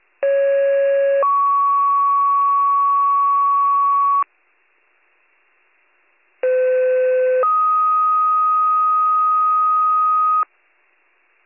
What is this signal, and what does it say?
Two-tone page, heard as a one-second tone, then a three-second tone, about 550 then 1120 Hz, about 510 then 1250 Hz. It is heard from 0.2 to 10.4 s.